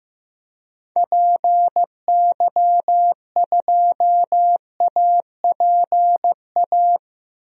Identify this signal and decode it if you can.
Morse code, heard as a keyed tone, about 16 wpm, 700 Hz. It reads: PY2APA
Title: DXCC-100 Call Signs - 6 Characters - 15wpm